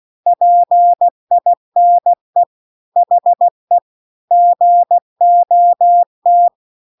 Morse code, heard as a keyed tone, about 16 wpm, 700 Hz. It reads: PINE HE GOT